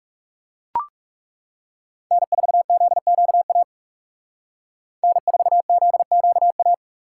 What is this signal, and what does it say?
Morse code, heard as a keyed tone, about 36 wpm, 700 Hz. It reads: N4ZQA N4ZQA
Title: Speed Racing - USA 1x3 Call Signs 30wpm